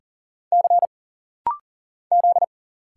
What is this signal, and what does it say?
Morse code, heard as a keyed tone, about 39 wpm, 700 Hz. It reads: C Z
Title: Single Letter-Number Alt - Rapid-Fire 40wpm